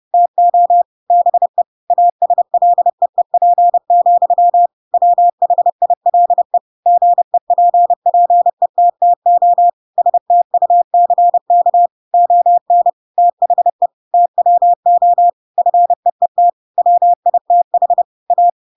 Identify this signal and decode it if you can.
Morse code, heard as a keyed tone, about 30 wpm, 700 Hz. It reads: TO BE ASLEEP, WHILE GEPPETTO STUCK ON THE TWO FEET WITH A
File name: Pinocchio / Pinocchio ch8 30 30 ik1hge